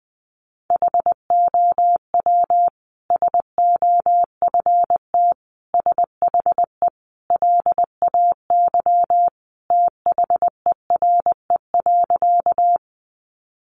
Morse code, heard as a keyed tone, about 20 wpm, 700 Hz. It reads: HOW SOFT SHE LAY THERE.
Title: Sentences from Top 1500 Words 20wpm